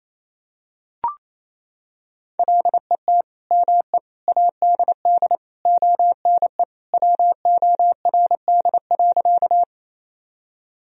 Morse code, heard as a keyed tone, about 28 wpm, 700 Hz. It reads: LET ME ADD ONE WORD.
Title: Sentences from Top 1000 Words - Review Part 1 - 28wpm